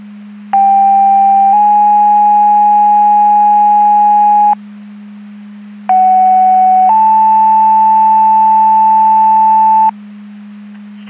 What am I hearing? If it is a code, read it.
Two-tone page, heard as a one-second tone, then a three-second tone, about 790 then 830 Hz, about 750 then 880 Hz. It is heard from 0.5 to 9.9 s.
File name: Portage County 154.3850 Fire EMS Paging